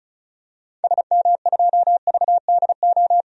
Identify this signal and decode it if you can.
Morse code, heard as a keyed tone, about 34 wpm, 700 Hz. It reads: SM2VDO
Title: DXCC-100 Call Signs - 6 Characters - 35wpm